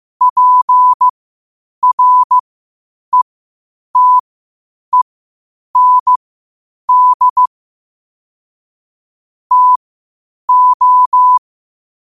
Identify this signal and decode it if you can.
Morse code, heard as a keyed tone, about 15 wpm, spaced as for 8 wpm, 1000 Hz. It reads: PRETEND TO
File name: qotdfiles / 8wpm